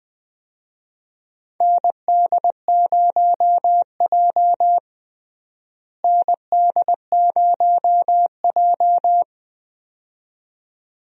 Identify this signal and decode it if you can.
Morse code, heard as a keyed tone, about 20 wpm, 700 Hz. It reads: ND0J ND0J
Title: USA Call Signs - 4 Characters - Repeated 2x - 20wpm